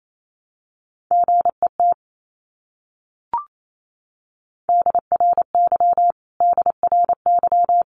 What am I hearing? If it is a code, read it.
Morse code, heard as a keyed tone, about 28 wpm, 700 Hz. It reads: GET DRY DRY
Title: Top 500 Words Repeated 28wpm